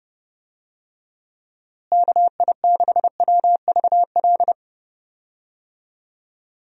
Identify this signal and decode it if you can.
Morse code, heard as a keyed tone, about 30 wpm, 700 Hz. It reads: KI6WVL
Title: USA Call Signs - 6 Characters - Repeated 3x - 30wpm